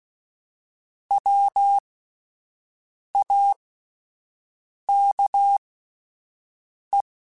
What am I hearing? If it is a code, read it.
Morse code, heard as a keyed tone, about 16 wpm, spaced as for 5 wpm, 800 Hz. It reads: WAKE
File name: Alphabetical / Alphabetical mp3s / Ws @5